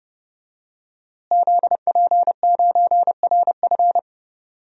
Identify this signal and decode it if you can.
Morse code, heard as a keyed tone, about 30 wpm, 700 Hz. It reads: ZP9RF
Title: DXCC-100 Call Signs - 5 Characters - Repeated 3x - 30wpm